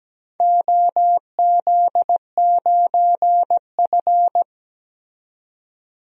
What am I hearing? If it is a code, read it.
Morse code, heard as a keyed tone, about 17 wpm, 700 Hz. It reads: OZ9F
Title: DXCC-100 Call Signs - 4 Characters - 17wpm